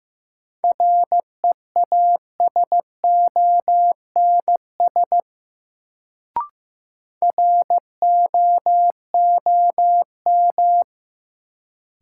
Morse code, heard as a keyed tone, about 15 wpm, 700 Hz. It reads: REASONS ROOM
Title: Sets of 4 Words - New Word Drill - 15wpm